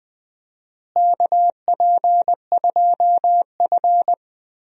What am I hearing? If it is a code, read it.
Morse code, heard as a keyed tone, about 20 wpm, 700 Hz. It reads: KP2F
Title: DXCC-100 Call Signs - 4 Characters - Repeated 3x - 20wpm